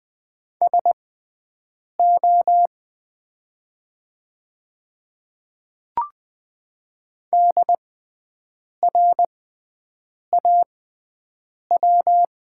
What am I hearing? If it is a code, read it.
Morse code, heard as a keyed tone, about 20 wpm, spaced as for 7 wpm, 700 Hz. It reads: SO DRAW
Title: 6x Character Spacing - Top 300 Words - 20wpm